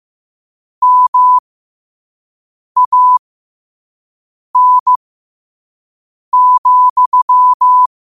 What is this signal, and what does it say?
Morse code, heard as a keyed tone, about 15 wpm, spaced as for 5 wpm, 1000 Hz. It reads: MAN,
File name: qotdfiles / 5wpm